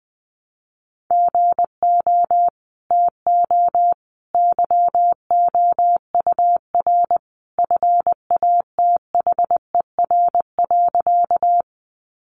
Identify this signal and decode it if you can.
Morse code, heard as a keyed tone, about 20 wpm, 700 Hz. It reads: GO TO YOUR FATHER.
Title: Sentences from Top 300 Words 20wpm